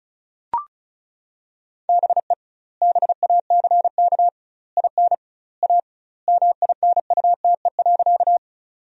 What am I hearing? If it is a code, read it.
Morse code, heard as a keyed tone, about 34 wpm, 700 Hz. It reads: BE BACK IN A MINUTE.
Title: 2x Word-Spacing - Sentences from Top 500 Words - 35wpm